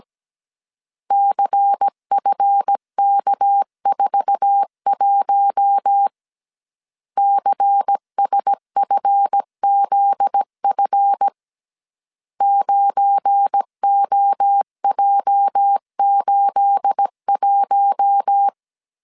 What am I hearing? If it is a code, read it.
Morse code, heard as a keyed tone, about 17 wpm, 800 Hz. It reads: CFK41 CSFZF 9OJ81
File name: CW / DET / 1 9 1500C S A 16wpm 2828